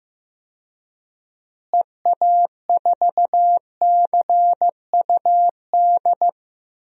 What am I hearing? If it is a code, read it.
Morse code, heard as a keyed tone, about 15 wpm, 700 Hz. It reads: EA4CUD
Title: DXCC-100 Call Signs - 6 Characters - Repeated 2x - 15wpm